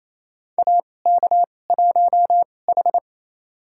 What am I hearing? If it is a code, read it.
Morse code, heard as a keyed tone, about 28 wpm, spaced as for 20 wpm, 700 Hz. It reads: AK1H
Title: USA Call Signs - 4 Characters - 2x Character Spacing - 28wpm